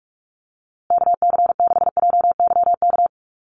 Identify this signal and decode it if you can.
Morse code, heard as a keyed tone, about 46 wpm, 690 Hz. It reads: KC6JYK
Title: USA Call Signs Repeated 3x 45wpm